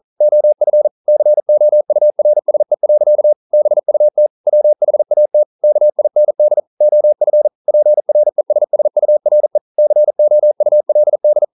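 Morse code, heard as a keyed tone, about 41 wpm, 600 Hz. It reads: OF COURSE. BUT WHAT KIND OF PRESSURE COULD
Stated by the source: Morse code at 40 wpm, practice text, CW med Gnister.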